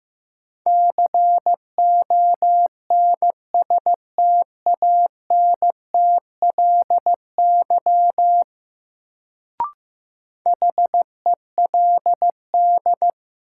Morse code, heard as a keyed tone, about 15 wpm, 700 Hz. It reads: CONSTANTLY HELD
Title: Sentences from Top 1600 Words - New Word Drill Alt - 15wpm